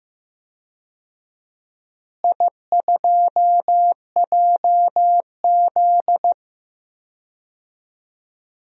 Morse code, heard as a keyed tone, about 15 wpm, 700 Hz. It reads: I2JZ
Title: DXCC-100 Call Signs - 4 Characters - Repeated 3x - 15wpm